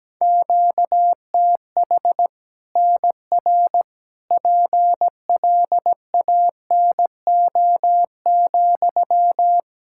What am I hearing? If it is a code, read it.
Morse code, heard as a keyed tone, about 17 wpm, 700 Hz. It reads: QTH NR PLANO,
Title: Practice QSO - Exchange 1 - Beginning - 17wpm